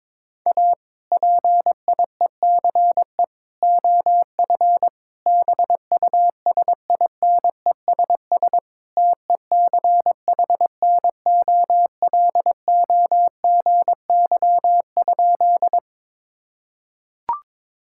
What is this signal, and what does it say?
Morse code, heard as a keyed tone, about 22 wpm, 700 Hz. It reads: A PIECE OF BUSINESS TECHNOLOGY?
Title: Sentences from Top 1000 Words - Review Part 5 - 22wpm